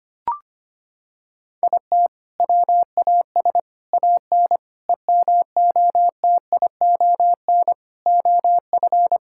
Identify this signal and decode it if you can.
Morse code, heard as a keyed tone, about 25 wpm, 700 Hz. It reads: IT WAS AN EMOTION OF
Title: Sentences from Top 1600 Words 25wpm